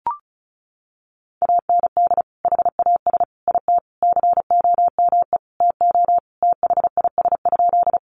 Morse code, heard as a keyed tone, about 34 wpm, 700 Hz. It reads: AND HAS IT COME TO THIS?
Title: Sentences from Top 100 Words 35wpm